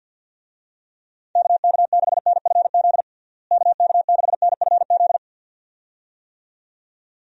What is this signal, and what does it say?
Morse code, heard as a keyed tone, about 53 wpm, 700 Hz. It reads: KK6NFZ KK6NFZ
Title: USA Call Signs - 6 Characters - Repeated 2x - 50wpm